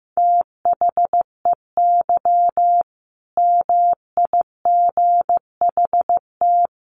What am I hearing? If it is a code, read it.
Morse code, heard as a keyed tone, about 14 wpm, 700 Hz. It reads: THEY MIGHT
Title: A Sound Of Thunder 15wpm